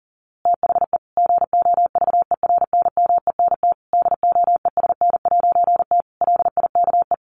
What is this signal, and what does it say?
Morse code, heard as a keyed tone, about 40 wpm, 700 Hz. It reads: THE GOVERNMENT DOESN'T LIKE
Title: A Sound Of Thunder 40wpm